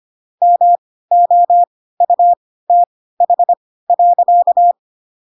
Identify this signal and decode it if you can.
Morse code, heard as a keyed tone, about 25 wpm, spaced as for 16 wpm, 700 Hz. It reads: MOUTH.
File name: Pinocchio / Pinocchio ch3 10 25 ik1hge